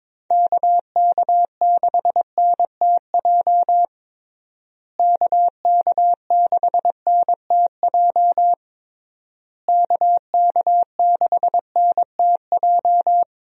Morse code, heard as a keyed tone, about 22 wpm, 700 Hz. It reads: KK6NTJ KK6NTJ KK6NTJ
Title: USA Call Signs - 6 Characters - Repeated 3x - 22wpm